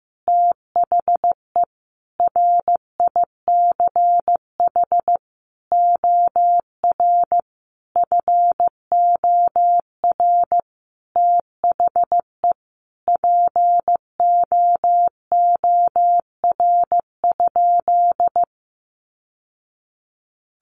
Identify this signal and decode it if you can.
Morse code, heard as a keyed tone, about 15 wpm, 700 Hz. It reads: THE RICH OR FOR THE POOR?
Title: Sentences from Top 1100 Words 15wpm